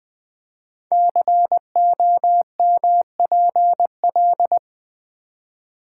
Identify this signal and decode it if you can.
Morse code, heard as a keyed tone, about 20 wpm, 700 Hz. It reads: COMPL
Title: Build A Word - 5 Letters At A Time - 12-Letter Words - 20wpm